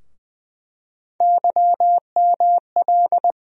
Morse code, heard as a keyed tone, about 21 wpm, 700 Hz. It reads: YML
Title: USA Call Sign Postfixes - 3 Letters 20wpm